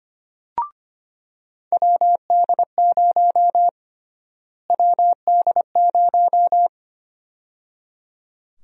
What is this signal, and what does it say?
Morse code, heard as a keyed tone, about 25 wpm, 700 Hz. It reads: WD0 WD0
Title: USA Call Sign Prefixes - Repeated 2x - 25wpm